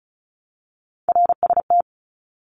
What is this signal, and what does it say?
Morse code, heard as a keyed tone, about 34 wpm, 700 Hz. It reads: RST
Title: Abbreviations 35wpm